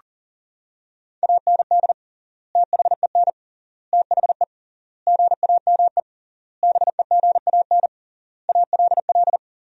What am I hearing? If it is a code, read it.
Morse code, heard as a keyed tone, about 40 wpm, 700 Hz. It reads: AND THEN THE GAME BEGAN ALL
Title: 3x Word-Spacing - Sentences from Top 500 Words - 40wpm